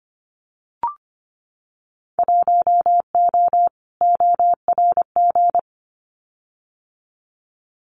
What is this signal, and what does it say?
Morse code, heard as a keyed tone, about 25 wpm, 700 Hz. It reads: 1O ORG
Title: Winter Field Day Exchanges 25wpm